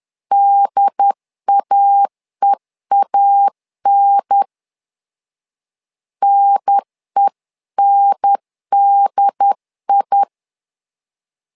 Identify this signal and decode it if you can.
Morse code, heard as a keyed tone, about 10 wpm, 800 Hz. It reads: DAEAN NENDI
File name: LEZIONI / 2 UHDN / 2 UHDN 10wpm 1904